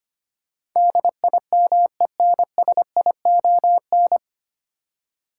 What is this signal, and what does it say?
Morse code, heard as a keyed tone, about 25 wpm, 700 Hz. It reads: DIMENSION
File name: Nine Letter Words - 25wpm